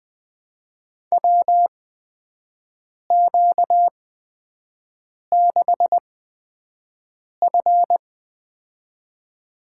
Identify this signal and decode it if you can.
Morse code, heard as a keyed tone, about 20 wpm, spaced as for 5 wpm, 700 Hz. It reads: WQ6F
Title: USA Call Signs - 4 Characters - 8x Character Spacing - 20wpm